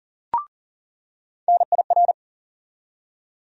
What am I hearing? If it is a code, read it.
Morse code, heard as a keyed tone, about 40 wpm, 700 Hz. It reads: NIR
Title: USA Call Sign Postfixes - 3 Letters 40wpm